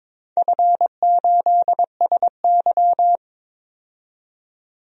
Morse code, heard as a keyed tone, about 22 wpm, 700 Hz. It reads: F8SY
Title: DXCC-100 Call Signs - 4 Characters - 22wpm